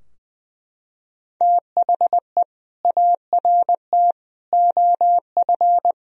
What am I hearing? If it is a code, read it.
Morse code, heard as a keyed tone, about 20 wpm, 700 Hz. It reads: THE ART OF
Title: Sentences from Top 1000 Words - Review Part 3 - 20wpm